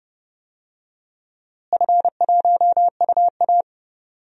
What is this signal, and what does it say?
Morse code, heard as a keyed tone, about 32 wpm, 700 Hz. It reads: F1UA
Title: DXCC-100 Call Signs - 4 Characters - Repeated 2x - 30wpm